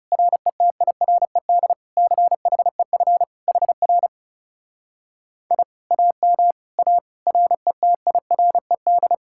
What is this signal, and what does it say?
Morse code, heard as a keyed tone, about 37 wpm, 700 Hz. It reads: RETIRED CHEF HR I AM A RETIRED
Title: Speed-Racing Common QSO Phrases - 30wpm